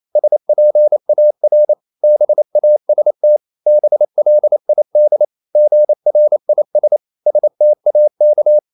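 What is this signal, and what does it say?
Morse code, heard as a keyed tone, about 28 wpm, 600 Hz. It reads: SPAR BAST BLID GRIS STAK
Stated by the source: Danish short words sent in Morse at 28 wpm, from CW med Gnister.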